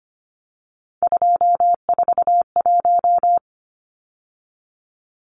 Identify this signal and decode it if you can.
Morse code, heard as a keyed tone, about 25 wpm, 700 Hz. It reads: 241
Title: Numbers - 3-Digit 25wpm